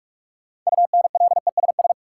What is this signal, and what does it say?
Morse code, heard as a keyed tone, about 46 wpm, 700 Hz. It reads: UNLESS
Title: Sentences from Top 1300 Words - New Word Drill Alt - 45wpm